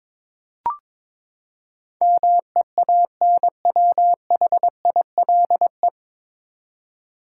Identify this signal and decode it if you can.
Morse code, heard as a keyed tone, about 21 wpm, 700 Hz. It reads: MEANWHILE
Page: Morse Code Ninja practice 22wpm